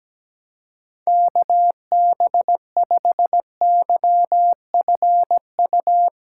Morse code, heard as a keyed tone, about 17 wpm, 700 Hz. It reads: KB5YFU
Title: USA Callsign DE Callsign 17wpm